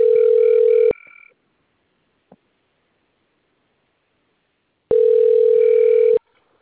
Ringback tone (North America), 440 + 480 Hz.
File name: out- - - - - .54691